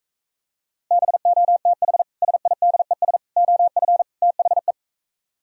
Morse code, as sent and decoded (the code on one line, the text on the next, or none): -... --- - .... / ... .. -.. ... / --- ..-. / - .... .
BOTH SIDES OF THE